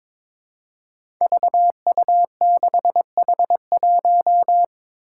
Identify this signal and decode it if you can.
Morse code, heard as a keyed tone, about 22 wpm, 700 Hz. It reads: VU6H1
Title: Five Challenging Letter-Number Combinations 22wpm